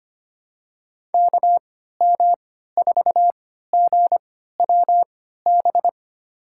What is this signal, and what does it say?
Morse code, heard as a keyed tone, about 25 wpm, spaced as for 14 wpm, 700 Hz. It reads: KM4GWB
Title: USA Call Signs - 6 Characters - 3x Character Spacing - 25wpm